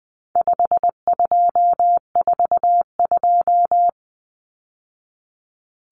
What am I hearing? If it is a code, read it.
Morse code, heard as a keyed tone, about 20 wpm, 700 Hz. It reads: 5242
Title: Numbers - 4-Digit 20wpm